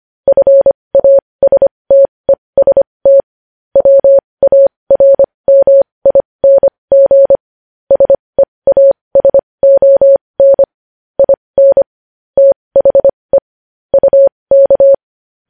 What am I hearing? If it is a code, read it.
Morse code, heard as a keyed tone, about 25 wpm, spaced as for 20 wpm, 550 Hz. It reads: FASTEST WARMING SEASON IN THE UK
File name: podcast / News headlines-1x-25x20